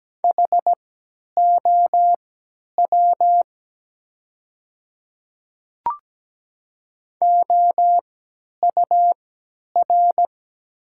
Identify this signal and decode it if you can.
Morse code, heard as a keyed tone, about 17 wpm, spaced as for 10 wpm, 700 Hz. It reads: HOW OUR
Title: 3x Character Spacing - Top 200 Words - 17wpm